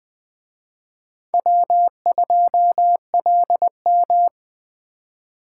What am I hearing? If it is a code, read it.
Morse code, heard as a keyed tone, about 20 wpm, 700 Hz. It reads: W2LM
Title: Speed Racing - USA 1x2 Call Signs 17wpm